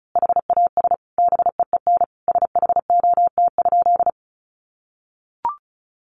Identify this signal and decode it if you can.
Morse code, heard as a keyed tone, about 34 wpm, 700 Hz. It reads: HAS BEEN SHOT?
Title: Sentences from Top 1100 Words 35wpm